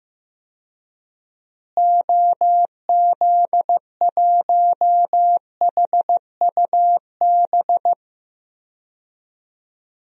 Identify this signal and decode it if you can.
Morse code, heard as a keyed tone, about 15 wpm, 700 Hz. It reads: OZ1HUB
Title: DXCC-100 Call Signs - 6 Characters - Repeated 2x - 15wpm